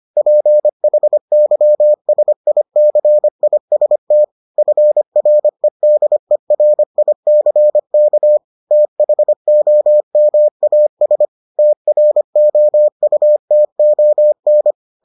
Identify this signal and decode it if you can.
Morse code, heard as a keyed tone, about 25 wpm, 600 Hz. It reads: PHYSICIST FREDERICK THOMAS TROUTON